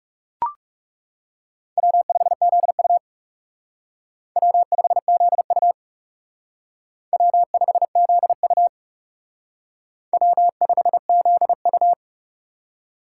Morse code, heard as a keyed tone, about 37 wpm, 700 Hz. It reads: W5ZU W5ZU W5ZU W5ZU
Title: Speed Racing - USA Call Signs 30wpm